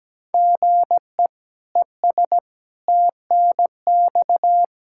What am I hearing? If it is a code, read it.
Morse code, heard as a keyed tone, about 17 wpm, 700 Hz. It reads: GE ES TNX
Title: Practice QSO - Exchange 1 - Beginning - 17wpm